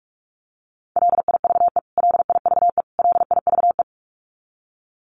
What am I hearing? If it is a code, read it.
Morse code, heard as a keyed tone, about 44 wpm, 700 Hz. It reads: LIVE LIVE LIVE
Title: Top 300 Words Repeated 45wpm